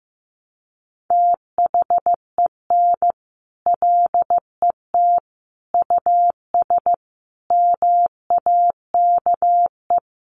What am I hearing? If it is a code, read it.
Morse code, heard as a keyed tone, about 15 wpm, 700 Hz. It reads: THEN LET US MAKE
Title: Sentences from Top 1200 Words 15wpm